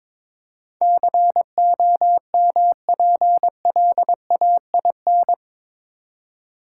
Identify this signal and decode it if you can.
Morse code, heard as a keyed tone, about 22 wpm, 700 Hz. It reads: COMPLAIN
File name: practice / 22wpm / Eight Letter Words - 22wpm